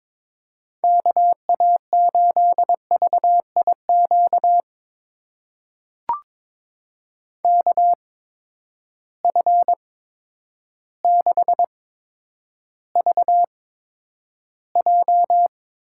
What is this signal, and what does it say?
Morse code, heard as a keyed tone, about 21 wpm, spaced as for 10 wpm, 700 Hz. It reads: KA8VIQ KF6VJ
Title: USA Call Signs - 6 Characters - 8x Character Spacing - 22wpm